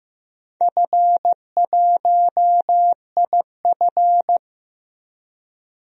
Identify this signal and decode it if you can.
Morse code, heard as a keyed tone, about 16 wpm, 700 Hz. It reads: F1IF
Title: DXCC-100 Call Signs - 4 Characters - Repeated 3x - 15wpm